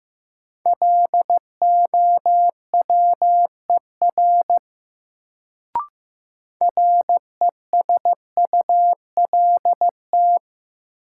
Morse code, heard as a keyed tone, about 15 wpm, 700 Hz. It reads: LOWER RESULT
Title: Sentences from Top 700 Words - New Word Drill - 15wpm